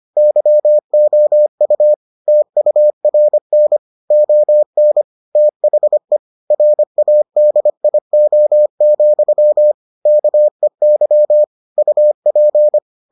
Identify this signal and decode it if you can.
Morse code, heard as a keyed tone, about 25 wpm, 600 Hz. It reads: YOU TURN ON THE RADIO, KEY UP